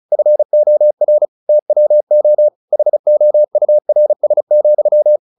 Morse code, heard as a keyed tone, about 34 wpm, 600 Hz. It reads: FOR TWO HOURS,
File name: the old man and the sea 2 35wpm